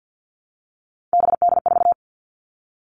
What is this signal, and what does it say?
Morse code, heard as a keyed tone, about 48 wpm, 700 Hz. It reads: BD4
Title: Callsign Prefixes for Top 100 DXCC - 50wpm